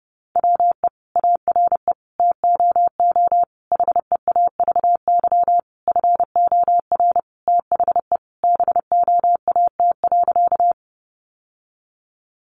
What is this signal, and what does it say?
Morse code, heard as a keyed tone, about 30 wpm, 700 Hz. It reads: WE ARE TOO HEAVY FOR THE BOAT.
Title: Sentences from Top 1100 Words 30wpm